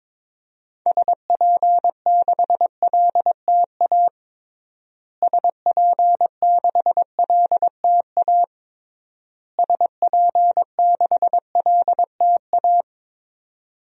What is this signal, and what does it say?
Morse code, heard as a keyed tone, about 22 wpm, 700 Hz. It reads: SP6LTA SP6LTA SP6LTA
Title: DXCC-100 Call Signs - 6 Characters - Repeated 3x - 22wpm